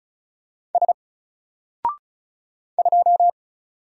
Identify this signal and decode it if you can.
Morse code, heard as a keyed tone, about 36 wpm, 700 Hz. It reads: S 2
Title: Single Letter-Number Alt 35wpm